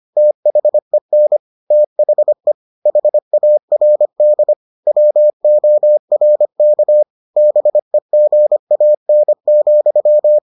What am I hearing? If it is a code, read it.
Morse code, heard as a keyed tone, about 25 wpm, 600 Hz. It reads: THEN THE HARD WORK BEGAN,